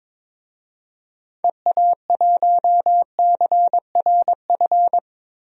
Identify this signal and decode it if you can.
Morse code, heard as a keyed tone, about 22 wpm, 700 Hz. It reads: EA1CRF
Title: DXCC-100 Call Signs - 6 Characters - Repeated 2x - 22wpm